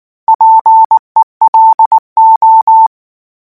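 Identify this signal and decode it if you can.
Morse code, heard as a keyed tone, about 19 wpm, 890 Hz. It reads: PELO
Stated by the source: Morse code practice text at 18 wpm, from Portal do CW.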